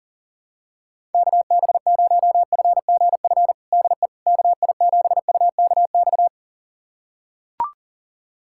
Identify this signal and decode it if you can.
Morse code, heard as a keyed tone, about 40 wpm, 700 Hz. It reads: KB0FGF DE KI7UKX
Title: USA Callsign DE Callsign - 6 Characters - 40wpm